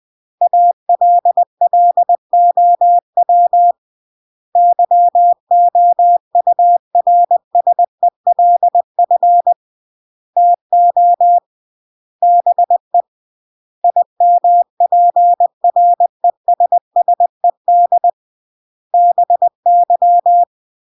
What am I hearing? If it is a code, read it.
Morse code, heard as a keyed tone, about 20 wpm, 700 Hz. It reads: ALLOW YOURSELF TO BE IMPRESSED BY
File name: CW / jodorowsky / 82 commandments 20-20